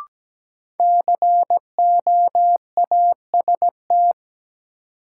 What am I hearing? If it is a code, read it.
Morse code, heard as a keyed tone, about 17 wpm, 700 Hz. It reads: COAST
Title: Sentences from Top 1000 Words - Review Part 4 - New Word Drill - 17wpm